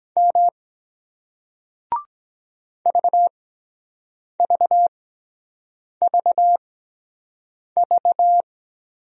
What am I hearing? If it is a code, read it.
Morse code, heard as a keyed tone, about 23 wpm, spaced as for 7 wpm, 700 Hz. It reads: M VVVV